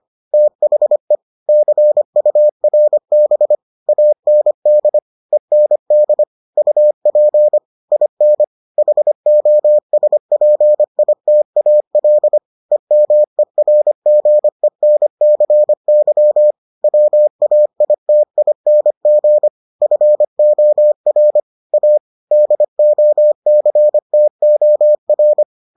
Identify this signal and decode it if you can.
Morse code, heard as a keyed tone, about 25 wpm, 600 Hz. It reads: THE CURB AND END UP IN HOSPITAL EMERGENCY WAITING FOR A DOCTOR